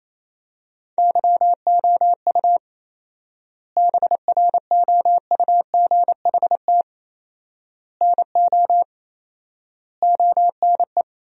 Morse code, heard as a keyed tone, about 28 wpm, 700 Hz. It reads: YOU BROUGHT NO ONE
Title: 4x Word-Spacing - Sentences from Top 500 Words - 28wpm